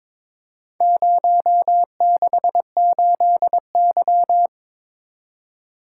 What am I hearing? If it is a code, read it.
Morse code, heard as a keyed tone, about 22 wpm, 700 Hz. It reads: 068Y
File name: practice / 22wpm / Four Letter-Number Combinations - 22wpm